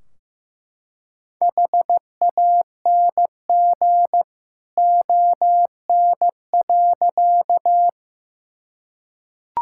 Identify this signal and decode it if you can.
Morse code, heard as a keyed tone, about 15 wpm, 700 Hz. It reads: HANG ON.
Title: Sentences from Top 1600 Words 15wpm